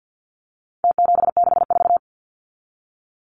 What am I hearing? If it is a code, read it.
Morse code, heard as a keyed tone, about 50 wpm, 700 Hz. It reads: T764
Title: Numbers - Contest Style - 4-Digit 50wpm